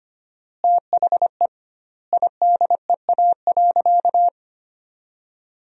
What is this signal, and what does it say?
Morse code, heard as a keyed tone, about 25 wpm, 700 Hz. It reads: THE IDEA.
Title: 2x Word-Spacing - Sentences from Top 300 Words - 25wpm